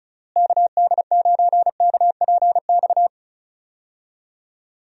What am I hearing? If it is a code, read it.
Morse code, heard as a keyed tone, about 34 wpm, 700 Hz. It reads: KD9KPX
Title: USA Call Signs - 6 Characters - Repeated 3x - 35wpm